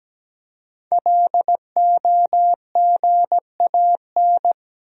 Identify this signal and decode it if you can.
Morse code, heard as a keyed tone, about 17 wpm, 700 Hz. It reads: LOGAN
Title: USA Common Names - 17wpm